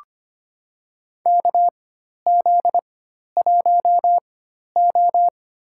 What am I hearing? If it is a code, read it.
Morse code, heard as a keyed tone, about 25 wpm, spaced as for 12 wpm, 690 Hz. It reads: KZ1O
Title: USA Call Signs - 4 Characters - 4x Character Spacing - 25wpm